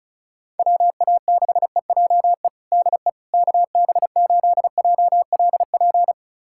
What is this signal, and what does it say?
Morse code, heard as a keyed tone, about 36 wpm, 700 Hz. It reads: WA6EJE DE KB8JLP
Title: USA Callsign DE Callsign - 6 Characters - 35wpm